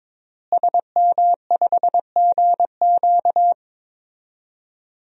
Morse code, heard as a keyed tone, about 22 wpm, 700 Hz. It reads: SM5GQ
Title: DXCC-100 Call Signs - 5 Characters - Repeated 3x - 22wpm